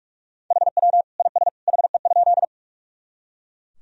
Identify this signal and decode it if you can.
Morse code, heard as a keyed tone, about 44 wpm, 700 Hz. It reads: HW IS HE?
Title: Sentences from Top 100 Words and Abbreviations 45wpm